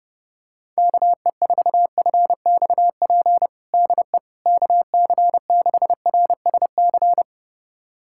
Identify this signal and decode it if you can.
Morse code, heard as a keyed tone, about 32 wpm, 700 Hz. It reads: KE4FXP DE KC6RSC
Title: USA Callsign DE Callsign - 6 Characters - 30wpm